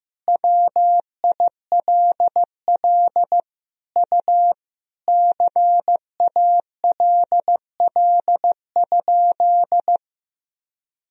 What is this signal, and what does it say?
Morse code, heard as a keyed tone, about 15 wpm, 700 Hz. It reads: WILL U CALL?
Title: Sentences from Top 100 Words and Abbreviations 15wpm